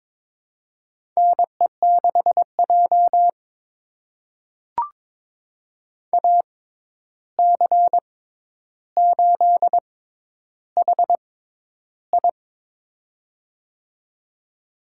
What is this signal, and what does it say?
Morse code, heard as a keyed tone, about 22 wpm, spaced as for 11 wpm, 700 Hz. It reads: NE6J AC8HI
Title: USA Call Signs - 6x Character Spacing - 22wpm